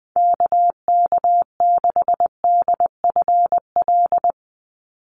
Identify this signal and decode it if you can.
Morse code, heard as a keyed tone, about 20 wpm, 700 Hz. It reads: KK6DFL
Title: USA Call Signs 20wpm